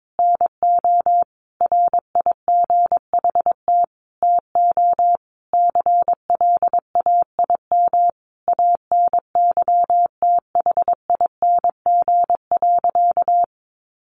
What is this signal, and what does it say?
Morse code, heard as a keyed tone, about 22 wpm, 700 Hz. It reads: NO RIGHT TO CLAIM ANYTHING.
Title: Sentences from Top 1500 Words 22wpm